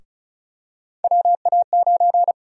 Morse code, heard as a keyed tone, about 35 wpm, 700 Hz. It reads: WA9
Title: USA Call Sign Prefixes - Repeated 2x - 35wpm